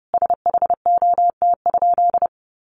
Morse code, heard as a keyed tone, about 30 wpm, 700 Hz. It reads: SHOT?
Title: Sentences from Top 1100 Words 30wpm